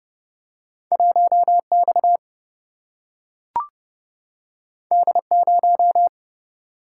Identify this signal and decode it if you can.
Morse code, heard as a keyed tone, about 30 wpm, 700 Hz. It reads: 1 X D 0
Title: Single-Letter Single-Number 30wpm